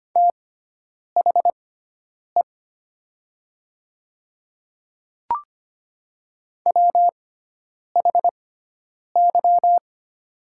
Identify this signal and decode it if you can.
Morse code, heard as a keyed tone, about 25 wpm, spaced as for 9 wpm, 700 Hz. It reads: THE WHY
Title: 6x Character Spacing - Top 500 Words - 25wpm